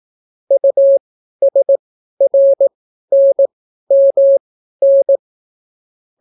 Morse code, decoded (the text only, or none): USRNMN